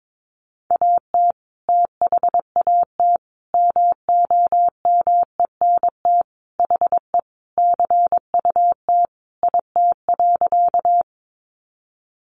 Morse code, heard as a keyed tone, about 22 wpm, 700 Hz. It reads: AT THAT MOMENT HE CUT IT.
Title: Sentences from Top 1000 Words 22wpm